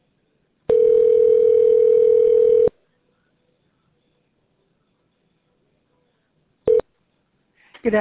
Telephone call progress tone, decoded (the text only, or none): ringback tone